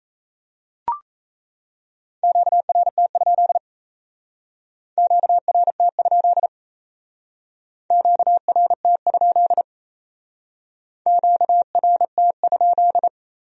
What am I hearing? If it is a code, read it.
Morse code, heard as a keyed tone, about 36 wpm, 700 Hz. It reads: QRT? QRT? QRT? QRT?